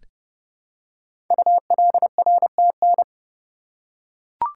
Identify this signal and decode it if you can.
Morse code, heard as a keyed tone, about 30 wpm, 690 Hz. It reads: ULRTN
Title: Code Groups - 5 Letters - 30wpm